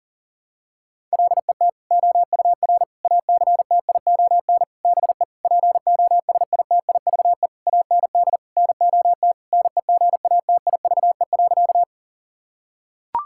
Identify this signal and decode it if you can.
Morse code, heard as a keyed tone, about 40 wpm, 700 Hz. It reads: LET OUR ACTION BE POSITIVE AND NOT NEGATIVE.
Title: Sentences from Top 1000 Words - Review Part 6 - 40wpm